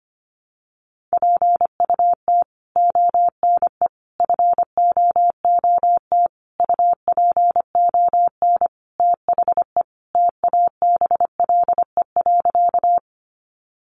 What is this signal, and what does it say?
Morse code, heard as a keyed tone, about 25 wpm, 700 Hz. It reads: PUT ONE FOOT UPON THE TABLE.
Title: Sentences from Top 1100 Words 25wpm